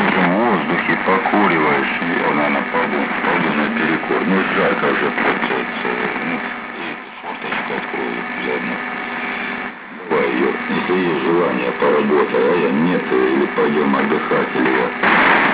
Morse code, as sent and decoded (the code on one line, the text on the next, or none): none